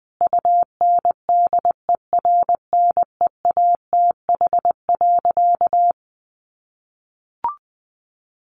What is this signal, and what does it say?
Morse code, heard as a keyed tone, about 19 wpm, 700 Hz. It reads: UNDERNEATH.
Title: A Sound Of Thunder 20wpm